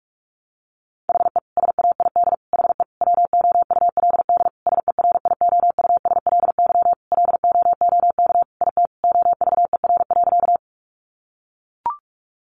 Morse code, heard as a keyed tone, about 44 wpm, 700 Hz. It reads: HE SAID HE WOULD SERIOUSLY LOOK IT OVER.
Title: Sentences from Top 1200 Words 45wpm